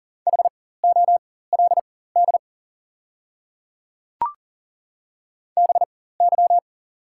Morse code, heard as a keyed tone, about 41 wpm, spaced as for 19 wpm, 690 Hz. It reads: HOLD BY
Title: 4x Character Spacing - Top 500 Words - 40wpm